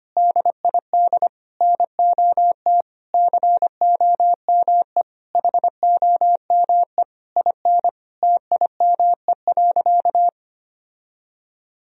Morse code, heard as a keyed tone, about 25 wpm, 700 Hz. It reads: DID NOT COME HOME IN TIME.
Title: Sentences from Top 1000 Words - Review Part 3 - 25wpm